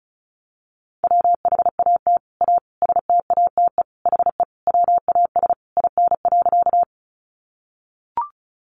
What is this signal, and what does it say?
Morse code, heard as a keyed tone, about 34 wpm, 700 Hz. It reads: WHAT A STATE HE WAS IN.
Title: Sentences from Top 400 Words 35wpm